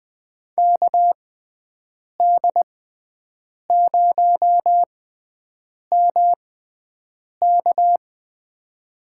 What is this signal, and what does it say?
Morse code, heard as a keyed tone, about 20 wpm, spaced as for 7 wpm, 700 Hz. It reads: KD0MK
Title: USA Call Signs - 5 Characters - 6x Character Spacing - 20wpm